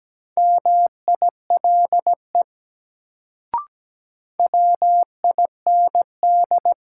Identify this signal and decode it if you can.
Morse code, heard as a keyed tone, about 17 wpm, 700 Hz. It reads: MILE WIND
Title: Sentences from Top 400 Words - New Word Drill - 17wpm